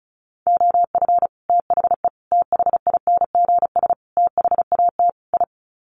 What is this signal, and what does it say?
Morse code, heard as a keyed tone, about 34 wpm, 700 Hz. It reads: OF THE THINGS THAT I
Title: Sets of 5 Words Encore 35wpm